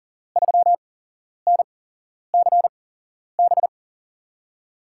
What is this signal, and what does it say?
Morse code, heard as a keyed tone, about 38 wpm, spaced as for 11 wpm, 700 Hz. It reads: 3NCB
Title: USA Call Signs - 5 Characters - 8x Character Spacing - 40wpm